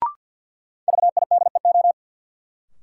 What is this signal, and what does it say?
Morse code, heard as a keyed tone, about 52 wpm, 700 Hz. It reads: VIDEO
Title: Sentences from Top 1000 Words - Review Part 4 - New Word Drill - 50wpm